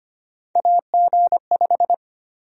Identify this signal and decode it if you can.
Morse code, heard as a keyed tone, about 25 wpm, 700 Hz. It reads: AG5
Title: USA Call Sign Prefixes - Repeated 2x - 25wpm